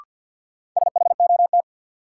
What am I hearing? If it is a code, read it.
Morse code, heard as a keyed tone, about 53 wpm, 700 Hz. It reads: SHOT
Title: Sentences from Top 1100 Words - New Word Drill Alt - 50wpm